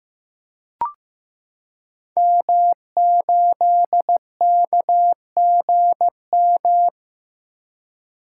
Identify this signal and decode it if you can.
Morse code, heard as a keyed tone, about 15 wpm, 700 Hz. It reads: M8KGM
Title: Code Groups - 5 Letter:Numbers - 15wpm